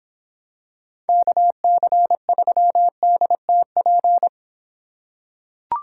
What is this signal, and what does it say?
Morse code, heard as a keyed tone, about 26 wpm, 700 Hz. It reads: KC3DTP